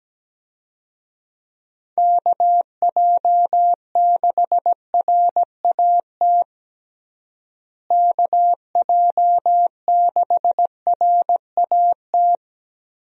Morse code, heard as a keyed tone, about 17 wpm, 700 Hz. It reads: KJ6RAT KJ6RAT
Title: USA Call Signs - 6 Characters - Repeated 3x - 17wpm